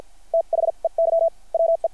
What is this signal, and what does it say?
Morse code, heard as a keyed tone, about 50 wpm, spaced as for 41 wpm, 660 Hz. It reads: THEY WE
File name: archived / CRSnet